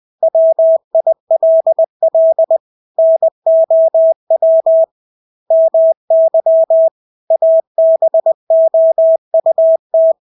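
Morse code, heard as a keyed tone, about 20 wpm, 650 Hz. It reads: WILL NOW MY ABOUT
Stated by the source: Morse practice of English words at 20 wpm.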